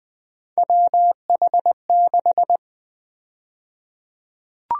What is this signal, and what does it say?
Morse code, heard as a keyed tone, about 20 wpm, 700 Hz. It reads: WH6
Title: USA Call Sign Prefixes - Repeated 2x - 20wpm